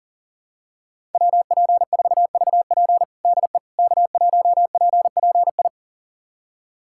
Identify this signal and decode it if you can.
Morse code, heard as a keyed tone, about 40 wpm, 700 Hz. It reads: WP4VP DE K1PPI
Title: USA Callsign DE Callsign - 5 Characters - 40wpm